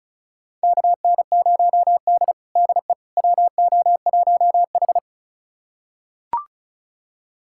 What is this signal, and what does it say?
Morse code, heard as a keyed tone, about 34 wpm, 700 Hz. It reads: KN0D DE WO1H
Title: USA Callsign DE Callsign - 4 Characters - 35wpm